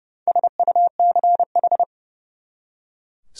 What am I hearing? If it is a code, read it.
Morse code, heard as a keyed tone, about 38 wpm, 700 Hz. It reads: SUCH